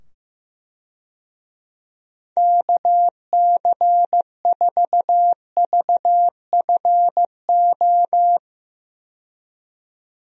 Morse code, heard as a keyed tone, about 15 wpm, 700 Hz. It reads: KC4VFO
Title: USA Call Signs - 6 Characters - 15wpm